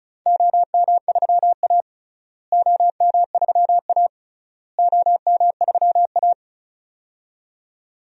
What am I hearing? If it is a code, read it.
Morse code, heard as a keyed tone, about 36 wpm, 700 Hz. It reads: OM3A OM3A OM3A
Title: DXCC-100 Call Signs - 4 Characters - Repeated 3x - 35wpm